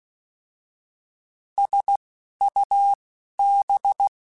Morse code, heard as a keyed tone, about 16 wpm, spaced as for 11 wpm, 800 Hz. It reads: SUB
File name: Alphabetical / Alphabetical mp3s / Ss @10